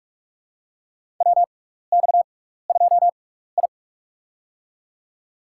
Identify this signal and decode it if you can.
Morse code, heard as a keyed tone, about 42 wpm, spaced as for 15 wpm, 700 Hz. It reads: WX2I
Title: USA Call Signs - 4 Characters - 6x Character Spacing - 45wpm